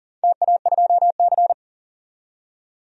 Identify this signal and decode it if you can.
Morse code, heard as a keyed tone, about 40 wpm, 700 Hz. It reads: TA2C
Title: DXCC-100 Call Signs - 4 Characters - 40wpm